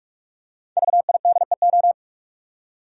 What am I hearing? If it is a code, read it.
Morse code, heard as a keyed tone, about 46 wpm, 700 Hz. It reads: VIDEO
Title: Five Letter Words - 45wpm